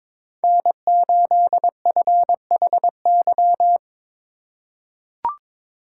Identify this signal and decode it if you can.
Morse code, heard as a keyed tone, about 22 wpm, 700 Hz. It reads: N8FHY
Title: USA Call Signs - 5 Characters - 3x Character Spacing - 22wpm